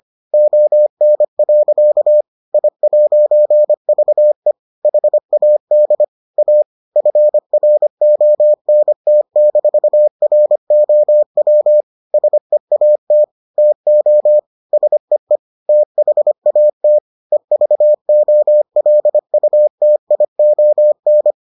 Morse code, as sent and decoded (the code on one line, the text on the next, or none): --- -. .-.-.- / .. .----. ...- . / .... .- -.. / .- / ..-. .-. --- -. - -....- .-. --- .-- / ... .- - / - --- / ... . / - .... .- - / . ...- --- .-.. ..- - .. --- -.
ON. I'VE HAD A FRONT-ROW SEAT TO SEE THAT EVOLUTION